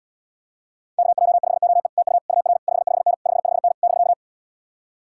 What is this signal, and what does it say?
Morse code, heard as a keyed tone, about 79 wpm, 700 Hz. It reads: KJ5OE TU UR 55N 55N <BK>